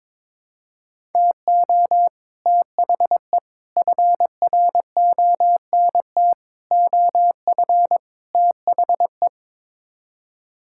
Morse code, heard as a keyed tone, about 22 wpm, 700 Hz. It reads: TO THE FRONT OF THE
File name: practice / 22wpm / Build A Phrase - 5-Word Phrases - 22wpm